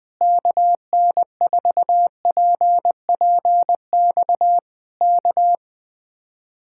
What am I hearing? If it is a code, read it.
Morse code, heard as a keyed tone, about 20 wpm, 700 Hz. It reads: KN4PPX K
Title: Speed-Racing Common QSO Phrases - 17wpm